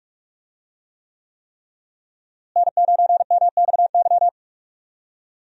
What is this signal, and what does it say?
Morse code, heard as a keyed tone, about 46 wpm, 700 Hz. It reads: N9MXY
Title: USA Call Signs - 5 Characters - Repeated 3x - 45wpm